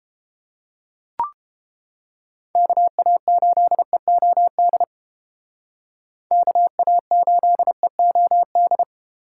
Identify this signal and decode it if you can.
Morse code, heard as a keyed tone, about 32 wpm, 700 Hz. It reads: KA8EOD KA8EOD